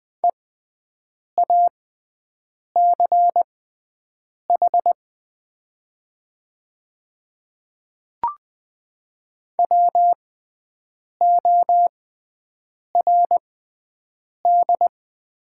Morse code, heard as a keyed tone, about 20 wpm, spaced as for 7 wpm, 700 Hz. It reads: EACH WORD
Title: 6x Character Spacing - Top 200 Words - 20wpm